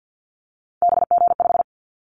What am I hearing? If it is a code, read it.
Morse code, heard as a keyed tone, about 50 wpm, 700 Hz. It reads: BG5
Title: Callsign Prefixes for Top 100 DXCC - 50wpm